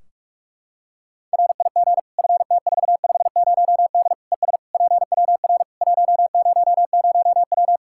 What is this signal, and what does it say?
Morse code, heard as a keyed tone, about 46 wpm, 700 Hz. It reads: RIG FT450D ES PWR 100W
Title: K1USN SST QSO Practice 45wpm